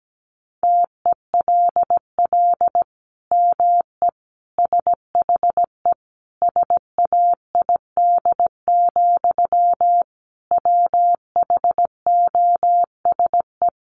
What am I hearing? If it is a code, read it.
Morse code, heard as a keyed tone, about 17 wpm, 700 Hz. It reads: TELL ME SHE SAID, WHOSE
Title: Sentences from Top 1200 Words 17wpm